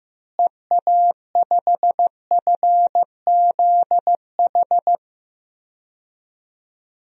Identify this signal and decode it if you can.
Morse code, heard as a keyed tone, about 16 wpm, 700 Hz. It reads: EA5FZH
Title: DXCC-100 Call Signs - 6 Characters - Repeated 2x - 15wpm